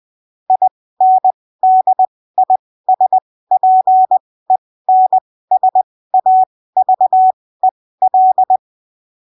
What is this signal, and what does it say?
Morse code, heard as a keyed tone, about 20 wpm, spaced as for 15 wpm, 750 Hz. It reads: INDISPENSAVEL
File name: texto 09 20ppm